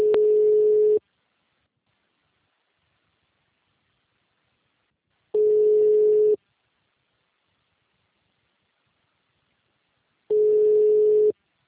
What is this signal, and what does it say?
Ringback tone (Europe, 425 Hz).